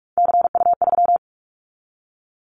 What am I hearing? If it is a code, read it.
Morse code, heard as a keyed tone, about 46 wpm, 700 Hz. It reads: CU3
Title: Callsign Prefixes for Top 50 DXCC - 45wpm